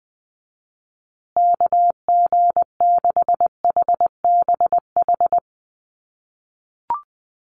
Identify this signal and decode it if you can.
Morse code, heard as a keyed tone, about 19 wpm, 700 Hz. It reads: KG6HBH
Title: USA Call Signs Repeated 2x 20wpm